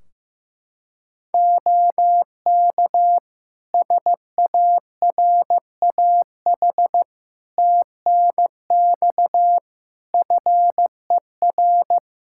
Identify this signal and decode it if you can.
Morse code, heard as a keyed tone, about 15 wpm, 700 Hz. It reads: OK SARAH TNX FER
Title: Practice QSO - Exchange 4 - Ending - 15wpm